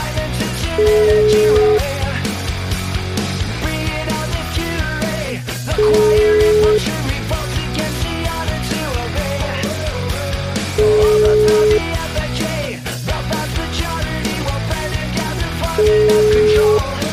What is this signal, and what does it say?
Ringback tone (Europe, 425 Hz), with noise.